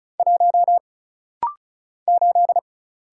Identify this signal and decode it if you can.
Morse code, heard as a keyed tone, about 35 wpm, 700 Hz. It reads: 1 8
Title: Single Letter-Number - Rapid-Fire 35wpm